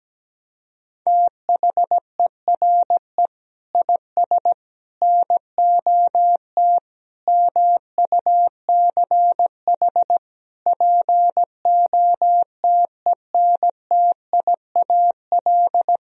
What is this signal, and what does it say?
Morse code, heard as a keyed tone, about 17 wpm, 700 Hz. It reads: THERE IS NOT MUCH POTENTIAL
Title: Sentences from Top 1000 Words - Review Part 4 - 17wpm